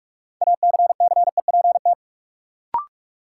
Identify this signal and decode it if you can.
Morse code, heard as a keyed tone, about 41 wpm, 700 Hz. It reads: ACCEPT
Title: Sentences from Top 1300 Words - New Word Drill - 45wpm